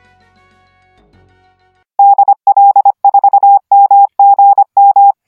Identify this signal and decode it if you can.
Morse code, heard as a keyed tone, about 25 wpm, 800 Hz. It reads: DL4MGM